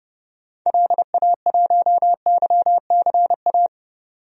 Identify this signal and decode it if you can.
Morse code, heard as a keyed tone, about 30 wpm, 700 Hz. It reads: LA1YCA
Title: DXCC-100 Call Signs - 6 Characters - 30wpm